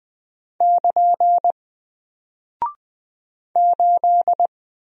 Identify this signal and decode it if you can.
Morse code, heard as a keyed tone, about 20 wpm, 700 Hz. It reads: <KN> 8
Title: Single Character (Letter:Number:Punctuation:Prosign) 20wpm